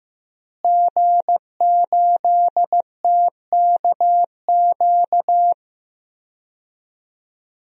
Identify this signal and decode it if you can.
Morse code, heard as a keyed tone, about 15 wpm, 700 Hz. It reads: G8TKQ
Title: DXCC-100 Call Signs - 5 Characters - Repeated 2x - 15wpm